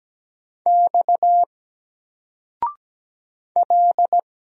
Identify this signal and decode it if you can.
Morse code, heard as a keyed tone, about 17 wpm, 700 Hz. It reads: X L
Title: Single Character (Letter:Number:Punctuation:Prosign) Alt 17wpm